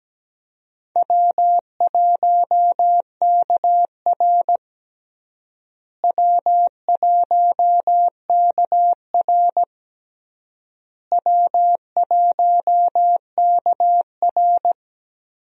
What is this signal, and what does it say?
Morse code, heard as a keyed tone, about 17 wpm, 700 Hz. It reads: W1KR W1KR W1KR
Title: USA Call Signs - 4 Characters - Repeated 3x - 17wpm